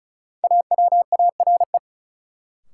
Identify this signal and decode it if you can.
Morse code, heard as a keyed tone, about 34 wpm, 700 Hz. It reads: AWARE
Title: Sentences from Top 1200 Words - New Word Drill Alt - 35wpm